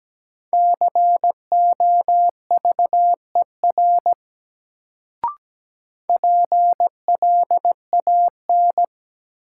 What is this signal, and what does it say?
Morse code, heard as a keyed tone, about 17 wpm, 700 Hz. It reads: COVER PLAN
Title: Sentences from Top 500 Words - New Word Drill - 17wpm